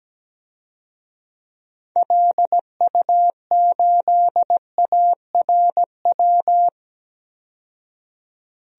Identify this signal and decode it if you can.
Morse code, heard as a keyed tone, about 17 wpm, 700 Hz. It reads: LU8ARW
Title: DXCC-100 Call Signs - 6 Characters - 17wpm